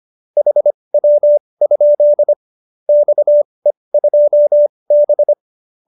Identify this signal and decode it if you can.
Morse code, heard as a keyed tone, about 25 wpm, spaced as for 20 wpm, 590 Hz. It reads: HW? XE2B